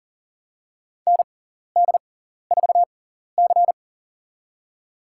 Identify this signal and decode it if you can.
Morse code, heard as a keyed tone, about 41 wpm, spaced as for 14 wpm, 690 Hz. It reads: ND4C
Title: USA Call Signs - 4 Characters - 6x Character Spacing - 40wpm